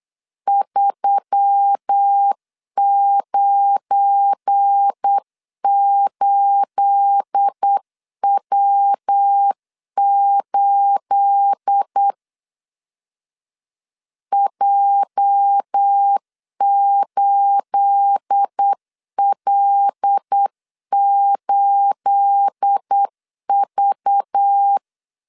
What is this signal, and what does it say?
Morse code, heard as a keyed tone, about 8 wpm, 800 Hz. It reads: 398W8 J8L8V
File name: LEZIONI / 9 28 8wpm 3512